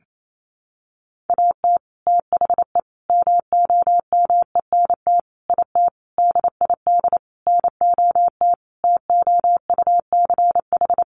Morse code, heard as a keyed tone, about 28 wpm, 700 Hz. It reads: AT THE MOMENT IT DID NOT TOUCH
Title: Sentences from Top 1200 Words 28wpm